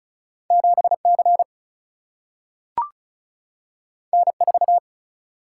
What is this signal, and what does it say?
Morse code, heard as a keyed tone, about 36 wpm, 700 Hz. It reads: 7C N4
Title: Single-Letter Single-Number 35wpm